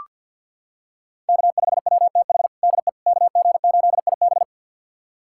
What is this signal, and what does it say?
Morse code, heard as a keyed tone, about 47 wpm, 700 Hz. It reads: K5WTH DE KG8IB
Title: USA Callsign DE Callsign - 5 Characters - 50wpm